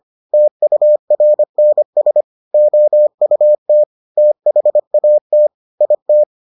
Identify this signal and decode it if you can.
Morse code, heard as a keyed tone, about 25 wpm, 600 Hz. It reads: TURNS OUT THAT IT